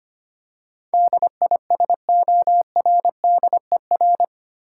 Morse code, heard as a keyed tone, about 25 wpm, 700 Hz. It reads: DISORDER
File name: Eight Letter Words - 25wpm